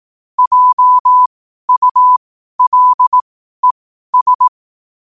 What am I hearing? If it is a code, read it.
Morse code, heard as a keyed tone, about 18 wpm, spaced as for 12 wpm, 1000 Hz. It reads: JULES
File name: qotdfiles / 12wpm